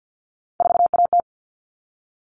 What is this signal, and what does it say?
Morse code, heard as a keyed tone, about 48 wpm, 700 Hz. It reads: 4AT
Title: All Cut Numbers - 3-Digit 50wpm